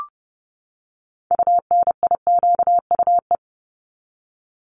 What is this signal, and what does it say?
Morse code, heard as a keyed tone, about 30 wpm, 700 Hz. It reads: UNIQUE
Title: Top 751-1000 Words 30wpm